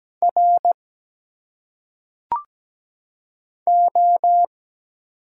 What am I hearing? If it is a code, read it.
Morse code, heard as a keyed tone, about 17 wpm, 700 Hz. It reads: R O
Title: Single Letters - Phonetic Pronunciation 17wpm